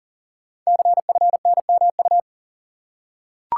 Morse code, heard as a keyed tone, about 40 wpm, 690 Hz. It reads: CFNMU
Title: Code Groups - 5 Letters - 40wpm